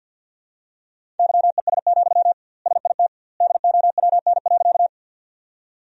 Morse code, heard as a keyed tone, about 48 wpm, 700 Hz. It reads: YES, SIT DOWN.
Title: 2x Word-Spacing - Sentences from Top 500 Words - 50wpm